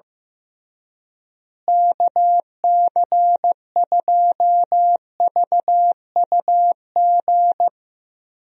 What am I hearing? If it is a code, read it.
Morse code, heard as a keyed tone, about 15 wpm, 700 Hz. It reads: KC2VUG
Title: USA Call Signs - 6 Characters - Repeated 3x - 15wpm